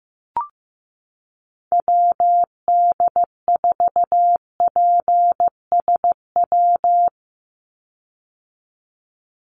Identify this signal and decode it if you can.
Morse code, heard as a keyed tone, about 15 wpm, 700 Hz. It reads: WD4PSW
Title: USA Call Signs 15wpm